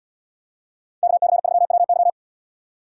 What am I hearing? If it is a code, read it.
Morse code, heard as a keyed tone, about 77 wpm, 700 Hz. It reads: KC3WY